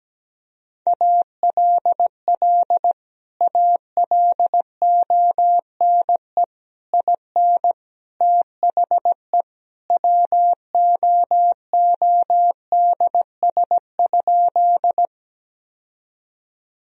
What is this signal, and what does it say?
Morse code, heard as a keyed tone, about 17 wpm, 700 Hz. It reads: ALL ALONE IN THE WOODS?
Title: Sentences from Top 1000 Words - Review Part 2 - 17wpm